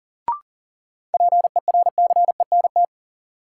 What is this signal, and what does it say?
Morse code, heard as a keyed tone, about 40 wpm, 700 Hz. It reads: PERCENT
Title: Sets of 2 Words - New Word Drill - 40wpm